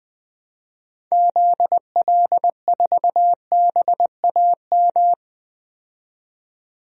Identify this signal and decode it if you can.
Morse code, heard as a keyed tone, about 20 wpm, 700 Hz. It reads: ZL4BAM
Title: DXCC-100 Call Signs - 6 Characters - 20wpm